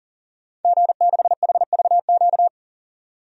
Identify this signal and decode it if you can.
Morse code, heard as a keyed tone, about 37 wpm, 700 Hz. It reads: G6HVQ
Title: DXCC-100 Call Signs - 5 Characters - 40wpm